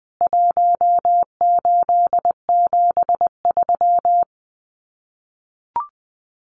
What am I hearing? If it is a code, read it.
Morse code, heard as a keyed tone, about 20 wpm, 700 Hz. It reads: 1873
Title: CWT Exchange 20wpm